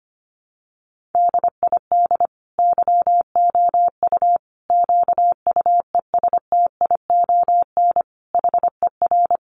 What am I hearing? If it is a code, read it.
Morse code, heard as a keyed tone, about 25 wpm, 700 Hz. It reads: DID YOU QUESTION HER
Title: Sentences from Top 800 Words 25wpm